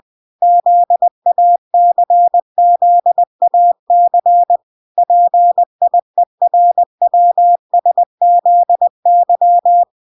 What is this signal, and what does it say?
Morse code, heard as a keyed tone, about 20 wpm, 700 Hz. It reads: ZACZAC PIERWSZY